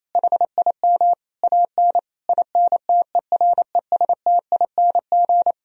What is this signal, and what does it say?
Morse code, heard as a keyed tone, about 28 wpm, 700 Hz. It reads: HIM AN INTERESTING
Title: Sentences from Top 1000 Words - Review Part 1 - 28wpm